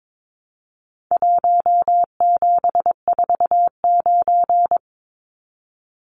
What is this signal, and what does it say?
Morse code, heard as a keyed tone, about 22 wpm, 700 Hz. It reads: 1749
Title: Numbers - Contest Style - 4-Digit 22wpm